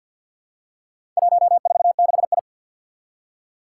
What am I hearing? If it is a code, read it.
Morse code, heard as a keyed tone, about 47 wpm, 700 Hz. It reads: 146I
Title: Four Challenging Letter-Number Combinations 50wpm